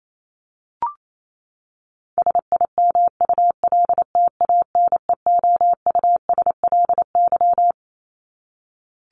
Morse code, heard as a keyed tone, about 28 wpm, 700 Hz. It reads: SIMULTANEOUSLY
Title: LY Suffix - Top 100 Words with Base Words 28wpm